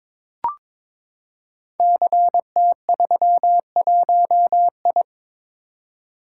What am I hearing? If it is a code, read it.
Morse code, heard as a keyed tone, about 22 wpm, 700 Hz. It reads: CT31I
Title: Code Groups - 5 Letter:Numbers - 22wpm